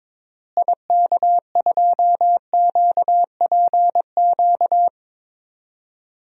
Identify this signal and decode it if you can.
Morse code, heard as a keyed tone, about 21 wpm, 700 Hz. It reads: IK2QPQ
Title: DXCC-100 Call Signs - 6 Characters - Repeated 2x - 22wpm